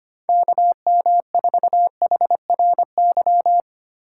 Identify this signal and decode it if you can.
Morse code, heard as a keyed tone, about 25 wpm, 700 Hz. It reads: KM4HRY
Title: USA Call Signs - 6 Characters - 6x Character Spacing - 25wpm